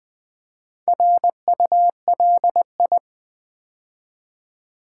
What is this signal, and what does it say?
Morse code, heard as a keyed tone, about 20 wpm, 700 Hz. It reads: RULI
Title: Build A Word - 2 Letters At A Time - 6-Letter Words - 20wpm